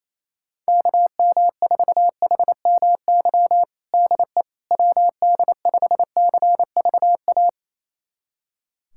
Morse code, heard as a keyed tone, about 28 wpm, 700 Hz. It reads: KM4HMY DE WD5CVA
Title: USA Callsign DE Callsign - 6 Characters - 28wpm